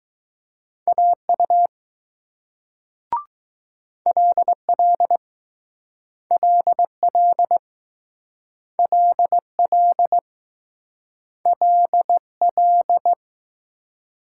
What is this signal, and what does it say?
Morse code, heard as a keyed tone, about 20 wpm, 700 Hz. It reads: AU LL LL LL LL